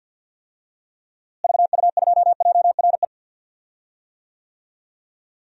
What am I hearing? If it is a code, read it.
Morse code, heard as a keyed tone, about 49 wpm, 700 Hz. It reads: VU2JRE
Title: DXCC-100 Call Signs - 6 Characters - Repeated 3x - 50wpm